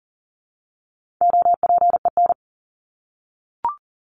Morse code, heard as a keyed tone, about 40 wpm, 700 Hz. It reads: OPEN
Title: ED Suffix - Top 100 Words with Base Words 40wpm